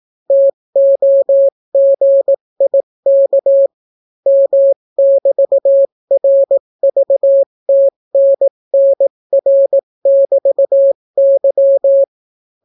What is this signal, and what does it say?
Morse code, heard as a keyed tone, about 18 wpm, 550 Hz. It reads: TOGIK M<BT>RVTNNR<BT>Y